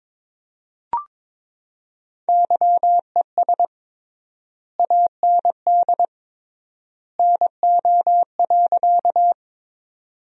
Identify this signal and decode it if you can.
Morse code, heard as a keyed tone, about 22 wpm, 700 Hz. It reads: YES AND NO.
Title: 3x Word-Spacing - Sentences from Top 500 Words - 22wpm